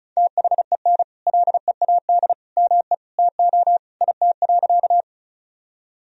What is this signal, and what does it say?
Morse code, heard as a keyed tone, about 36 wpm, 700 Hz. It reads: THEN LEAD ME TO IT.
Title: Sentences from Top 1000 Words - Review Part 2 - 35wpm